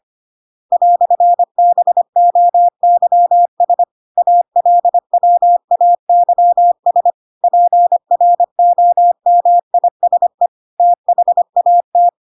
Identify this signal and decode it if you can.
Morse code, heard as a keyed tone, about 25 wpm, 700 Hz. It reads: "BOYS ALWAYS PROMISE THAT
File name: Pinocchio / Pinocchio ch8 25 25 ik1hge